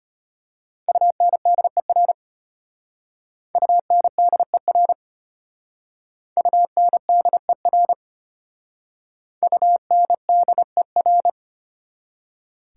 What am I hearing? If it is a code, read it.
Morse code, heard as a keyed tone, about 34 wpm, 700 Hz. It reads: UNDER UNDER UNDER UNDER